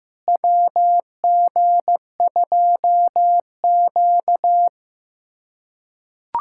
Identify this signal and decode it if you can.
Morse code, heard as a keyed tone, about 15 wpm, 700 Hz. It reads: WG2Q
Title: USA Callsign DE Callsign - 4 Characters - 15wpm